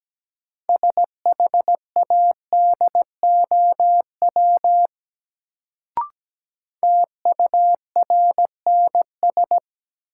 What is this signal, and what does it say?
Morse code, heard as a keyed tone, about 17 wpm, 700 Hz. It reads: SHADOW TURNS
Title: Sets of 5 Words Encore - New Word Drill - 17wpm